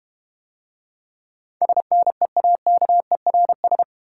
Morse code, heard as a keyed tone, about 32 wpm, 700 Hz. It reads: SNEAKERS